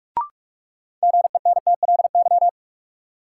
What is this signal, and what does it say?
Morse code, heard as a keyed tone, about 46 wpm, 700 Hz. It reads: GENTLY
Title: Sentences from Top 1500 Words - New Word Drill Alt - 45wpm